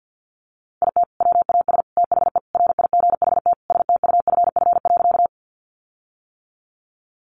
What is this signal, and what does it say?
Morse code, heard as a keyed tone, about 50 wpm, 700 Hz. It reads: IT WAS THE RIGHT STUFF.
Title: Sentences from Top 1400 Words 50wpm